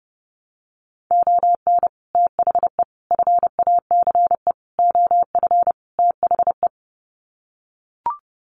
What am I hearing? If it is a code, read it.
Morse code, heard as a keyed tone, about 30 wpm, 700 Hz. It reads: ON THE FACE OF THE
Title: Sets of 5 Words Encore2 30wpm